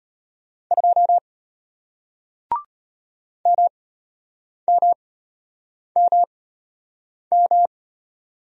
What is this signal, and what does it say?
Morse code, heard as a keyed tone, about 33 wpm, spaced as for 8 wpm, 700 Hz. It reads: 2 MMMM